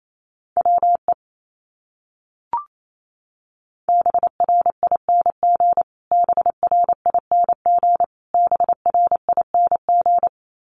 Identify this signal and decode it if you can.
Morse code, heard as a keyed tone, about 28 wpm, 700 Hz. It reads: WE BRING BRING BRING
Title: Top 500 Words Repeated 28wpm